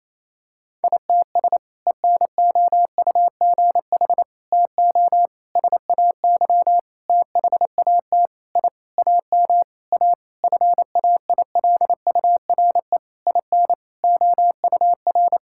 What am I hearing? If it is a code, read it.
Morse code, heard as a keyed tone, about 28 wpm, 700 Hz. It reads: ITS ENOUGH TO SAY THAT I AM A FAILURE IN OUR
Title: Sentences from Top 1000 Words - Review Part 6 - 28wpm